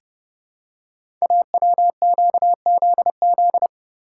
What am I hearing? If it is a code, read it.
Morse code, heard as a keyed tone, about 30 wpm, 700 Hz. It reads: AWQZZ
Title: Code Groups - 5 Letters - 30wpm